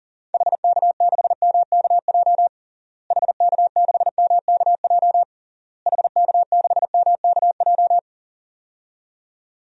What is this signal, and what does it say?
Morse code, heard as a keyed tone, about 37 wpm, 700 Hz. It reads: HK6MKJ HK6MKJ HK6MKJ
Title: DXCC-100 Call Signs - 6 Characters - Repeated 3x - 40wpm